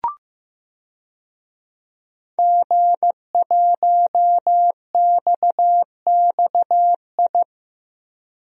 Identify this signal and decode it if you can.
Morse code, heard as a keyed tone, about 16 wpm, 700 Hz. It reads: G1XXI
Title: DXCC-100 Call Signs - 5 Characters - Repeated 2x - 15wpm